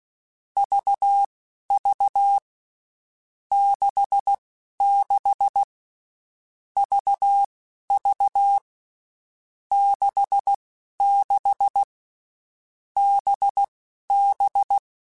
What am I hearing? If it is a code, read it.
Morse code, heard as a keyed tone, about 16 wpm, spaced as for 11 wpm, 800 Hz. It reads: VV 66 VV 66 BB
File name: Groups@10 / Groups@10 mp3s / VB46@10